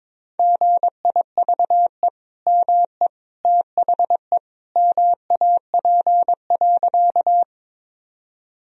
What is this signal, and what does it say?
Morse code, heard as a keyed tone, about 22 wpm, 700 Hz. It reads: GIVE ME THE MAP.
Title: Sentences from Top 1000 Words - Review Part 2 - 22wpm